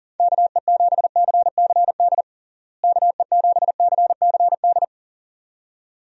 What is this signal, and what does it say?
Morse code, heard as a keyed tone, about 40 wpm, 700 Hz. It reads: KE7CCD KE7CCD
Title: USA Call Signs - 6 Characters - Repeated 2x - 40wpm